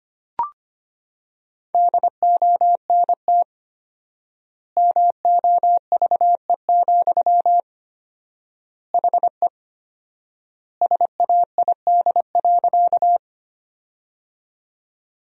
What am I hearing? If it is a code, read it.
Morse code, heard as a keyed tone, about 25 wpm, 700 Hz. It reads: DONT MOVE, HE SAID.
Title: 4x Word-Spacing - Sentences from Top 300 Words - 25wpm